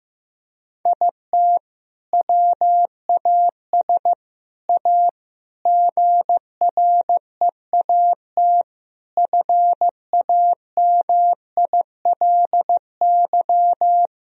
Morse code, heard as a keyed tone, about 16 wpm, 700 Hz. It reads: IT WAS A GREAT FAMILY
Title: Sentences from Top 1000 Words - Review Part 2 - 15wpm